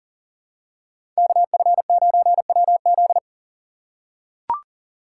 Morse code, heard as a keyed tone, about 37 wpm, 700 Hz. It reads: KF9WZ
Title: USA Call Signs - 4x Character Spacing - 40wpm